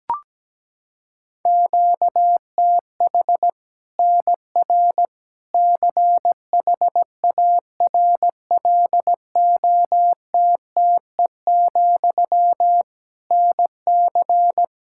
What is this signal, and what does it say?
Morse code, heard as a keyed tone, about 17 wpm, 700 Hz. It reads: QTH NR CHARLOTTE, NC
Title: Practice QSO - Exchange 1 - Beginning - 17wpm